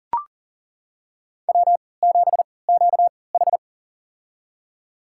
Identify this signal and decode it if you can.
Morse code, heard as a keyed tone, about 37 wpm, spaced as for 22 wpm, 700 Hz. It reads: W7QH
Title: USA Call Signs - 3x Character Spacing - 40wpm